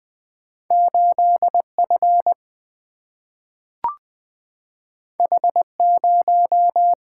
Morse code, heard as a keyed tone, about 20 wpm, 700 Hz. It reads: 8 F H 0
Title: Single-Letter Single-Number 20wpm